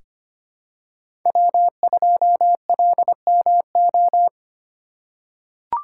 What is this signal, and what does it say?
Morse code, heard as a keyed tone, about 25 wpm, 700 Hz. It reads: W2LMO
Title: USA Call Signs - 5 Characters - 3x Character Spacing - 25wpm